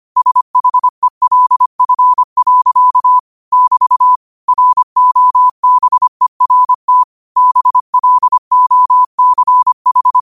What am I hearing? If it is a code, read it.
Morse code, heard as a keyed tone, about 25 wpm, 1000 Hz. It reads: SHELF. <BT> ROBERT BLOCH